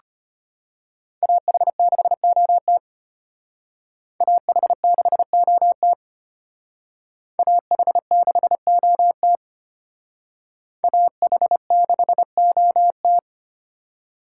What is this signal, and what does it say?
Morse code, heard as a keyed tone, about 29 wpm, 700 Hz. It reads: AH6OT AH6OT AH6OT AH6OT
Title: Speed Racing - USA 2x2 Call Signs 25wpm